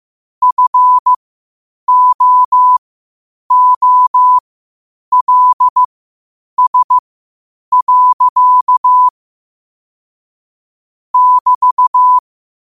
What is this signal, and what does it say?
Morse code, heard as a keyed tone, about 15 wpm, spaced as for 8 wpm, 1000 Hz. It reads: FOOLS. <BT>
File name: qotdfiles / 8wpm